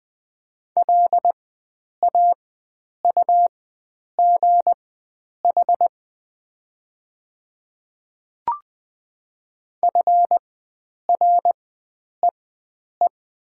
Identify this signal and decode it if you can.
Morse code, heard as a keyed tone, about 20 wpm, spaced as for 9 wpm, 700 Hz. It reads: LAUGH FREE
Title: 4x Character Spacing - Top 500 Words - 20wpm